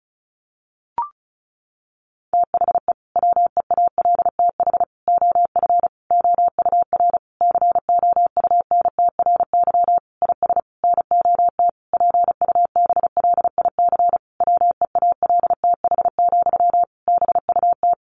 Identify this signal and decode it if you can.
Morse code, heard as a keyed tone, about 34 wpm, 700 Hz. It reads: THE WEALTH OF OUR COUNTRY IS NOT PUBLIC WEALTH, BUT
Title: Sentences from Top 1200 Words 35wpm